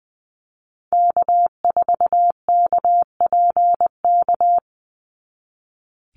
Morse code, heard as a keyed tone, about 20 wpm, 700 Hz. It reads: K4KPK
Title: NA SOTA Activator Callsigns 20wpm